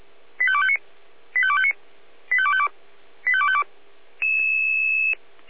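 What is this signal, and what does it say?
Five-tone selective call (ZVEI), heard as quick runs of five tones, tones of about 70 ms. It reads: 85259 85259 85252 85252
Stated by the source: Five-tone ZVEI selective call received on an SDR.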